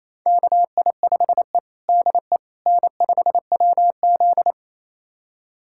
Morse code, heard as a keyed tone, about 27 wpm, 700 Hz. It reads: KI5E DE N5WZ
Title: USA Callsign DE Callsign - 4 Characters - 28wpm